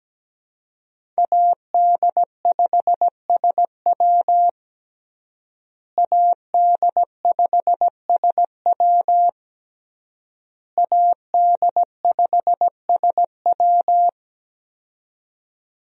Morse code, heard as a keyed tone, about 17 wpm, 700 Hz. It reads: AD5SW AD5SW AD5SW
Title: USA Call Signs - 5 Characters - Repeated 3x - 17wpm